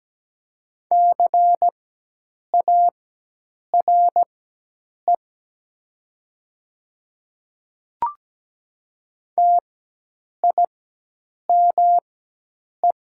Morse code, heard as a keyed tone, about 17 wpm, spaced as for 8 wpm, 690 Hz. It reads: CARE TIME
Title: 4x Character Spacing - Top 500 Words - 17wpm